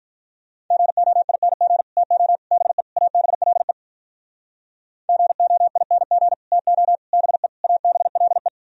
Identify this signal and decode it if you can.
Morse code, heard as a keyed tone, about 46 wpm, 700 Hz. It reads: GOING TO BE ABLE GOING TO BE ABLE